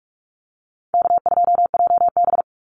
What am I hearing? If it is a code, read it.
Morse code, heard as a keyed tone, about 44 wpm, 700 Hz. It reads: K2JB
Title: NA SOTA Chaser Callsigns 45wpm